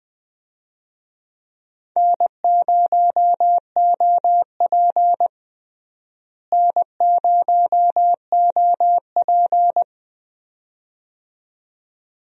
Morse code, heard as a keyed tone, about 20 wpm, 700 Hz. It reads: N0OP N0OP
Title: USA Call Signs - 4 Characters - Repeated 2x - 20wpm